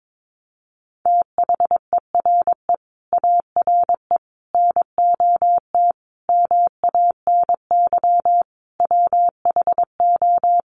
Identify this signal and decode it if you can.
Morse code, heard as a keyed tone, about 22 wpm, 700 Hz. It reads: THERE ARE NOT MANY WHO
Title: Sentences from Top 1300 Words 22wpm